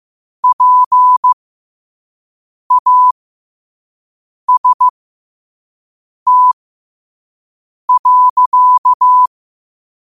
Morse code, heard as a keyed tone, about 15 wpm, spaced as for 5 wpm, 1000 Hz. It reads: PAST.
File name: qotdfiles / 5wpm